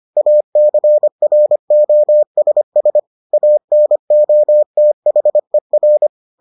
Morse code, heard as a keyed tone, about 25 wpm, 600 Hz. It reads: ACROSS ANOTHER